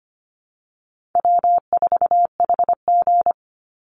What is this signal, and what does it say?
Morse code, heard as a keyed tone, about 25 wpm, 700 Hz. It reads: W4HG
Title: Easy Call Signs Repeated 2x - 25wpm